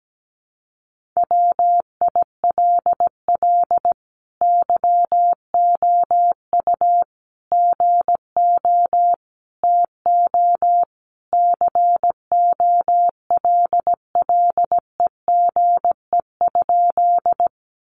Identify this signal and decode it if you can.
Morse code, heard as a keyed tone, about 17 wpm, 700 Hz. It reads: WILL YOU GO TO COLLEGE?
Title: Sentences from Top 1300 Words 17wpm